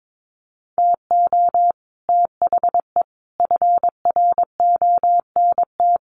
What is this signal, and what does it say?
Morse code, heard as a keyed tone, about 22 wpm, 700 Hz. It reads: TO THE FRONT
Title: Sets of 5 Words 22wpm